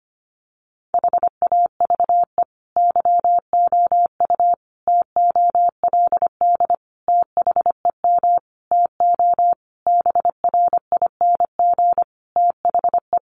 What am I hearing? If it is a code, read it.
Morse code, heard as a keyed tone, about 25 wpm, 700 Hz. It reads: HAVE YOU TOLD THEM TO BRING THE
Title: Sentences from Top 500 Words 25wpm